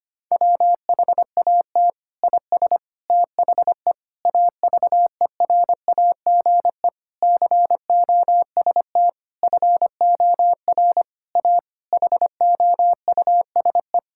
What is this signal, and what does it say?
Morse code, heard as a keyed tone, about 25 wpm, 700 Hz. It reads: WHAT IS THE AVERAGE COST FOR A HOUSE
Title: Sentences from Top 1000 Words - Review Part 6 - 25wpm